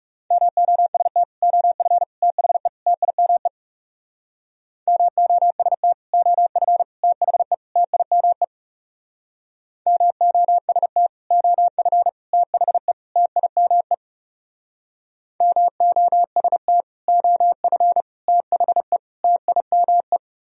Morse code, heard as a keyed tone, about 37 wpm, 700 Hz. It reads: MOST OF THE TIME MOST OF THE TIME MOST OF THE TIME MOST OF THE TIME
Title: Speed-Racing - Sets of 4 Words 30wpm